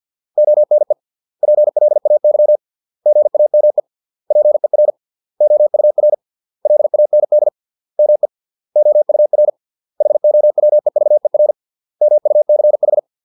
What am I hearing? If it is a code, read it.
Morse code, heard as a keyed tone, about 50 wpm, 600 Hz. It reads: ONE PLAY GAME PER OUR LAND ME OUR HOWEVER MUCH